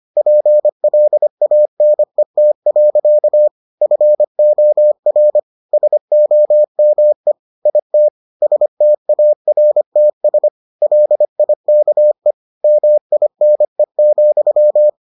Morse code, heard as a keyed tone, about 25 wpm, 600 Hz. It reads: PLANET. FOR SOME IT STARTS LIKE MINE,